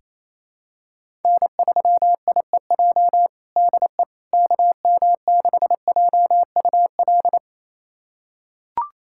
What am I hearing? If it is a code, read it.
Morse code, heard as a keyed tone, about 28 wpm, 700 Hz. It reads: N3IEJ DE KM6JUL
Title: USA Callsign DE Callsign 28wpm